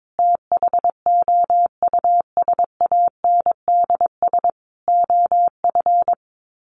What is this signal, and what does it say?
Morse code, heard as a keyed tone, about 22 wpm, 700 Hz. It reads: THOUSANDS OF
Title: Sets of 4 Words 22wpm